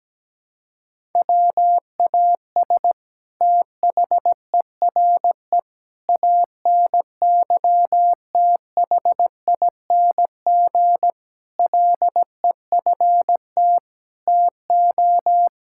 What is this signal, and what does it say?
Morse code, heard as a keyed tone, about 17 wpm, 700 Hz. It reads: WAS THERE ANYTHING LEFT TO
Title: Sentences from Top 1000 Words - Review Part 6 - 17wpm